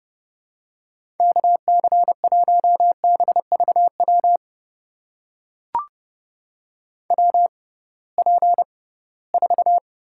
Morse code, heard as a keyed tone, about 30 wpm, spaced as for 19 wpm, 700 Hz. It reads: KC1BVW WP4
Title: USA Call Signs - 6 Characters - 6x Character Spacing - 30wpm